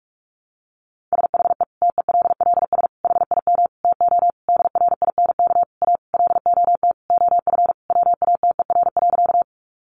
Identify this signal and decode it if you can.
Morse code, heard as a keyed tone, about 44 wpm, 700 Hz. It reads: SHE TELLS HIM TO DRINK A LOT OF WATER.
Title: Additional Easy Phrases 45wpm